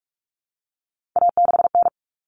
Morse code, heard as a keyed tone, about 44 wpm, 700 Hz. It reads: A6N
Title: All Cut Numbers - 3-Digit 45wpm